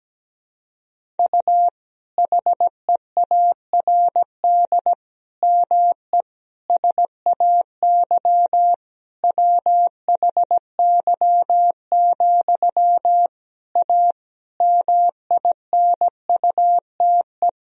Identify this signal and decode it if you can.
Morse code, heard as a keyed tone, about 17 wpm, 700 Hz. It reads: U HEARD ME SAY WHY, A MINUTE
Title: Sentences from Top 500 Words and Abbreviations 17wpm